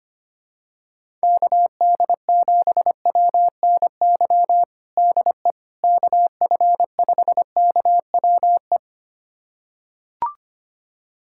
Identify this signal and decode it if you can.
Morse code, heard as a keyed tone, about 25 wpm, 700 Hz. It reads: KD7WNY DE KF5KWE
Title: USA Callsign DE Callsign - 6 Characters - 25wpm